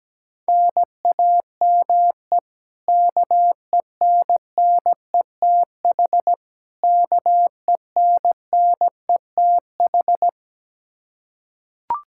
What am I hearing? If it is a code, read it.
Morse code, heard as a keyed tone, about 17 wpm, 700 Hz. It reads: NAME KENNETH KENNETH
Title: Practice QSO - Exchange 1 - Beginning - 17wpm